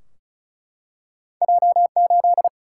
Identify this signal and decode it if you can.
Morse code, heard as a keyed tone, about 36 wpm, 700 Hz. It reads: J8
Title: Single-Letter Single-Number 35wpm